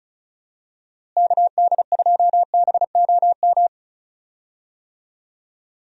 Morse code, heard as a keyed tone, about 36 wpm, 700 Hz. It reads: KD2BOM
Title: USA Call Signs - 6 Characters - Repeated 3x - 35wpm